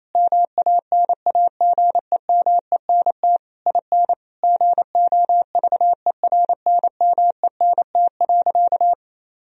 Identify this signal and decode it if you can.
Morse code, heard as a keyed tone, about 28 wpm, 700 Hz. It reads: MANAGEMENT IN GOVERNMENT.
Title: Sentences from Top 1000 Words - Review Part 4 - 28wpm